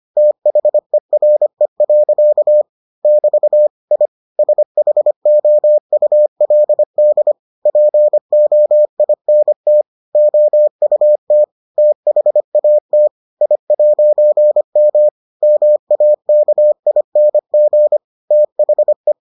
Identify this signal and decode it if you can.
Morse code, heard as a keyed tone, about 25 wpm, 600 Hz. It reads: THERE. <BT> I SHOULD POINT OUT THAT I'M MAKING THE